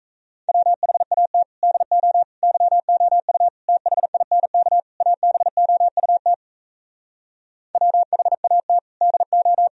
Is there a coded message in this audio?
Yes: Morse code from 0.5 to 9.7 s, heard as a keyed tone, about 38 wpm, 700 Hz. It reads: WHAT DO YOU THINK ABOUT WHAT DO